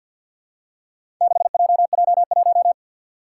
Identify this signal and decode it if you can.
Morse code, heard as a keyed tone, about 52 wpm, 700 Hz. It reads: 6JJ1
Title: Four Challenging Letter-Number Combinations 50wpm